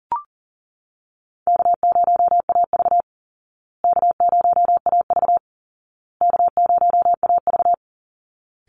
Morse code, heard as a keyed tone, about 40 wpm, 700 Hz. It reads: K0AV K0AV K0AV
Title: Easy Call Signs Repeated 3x - 40wpm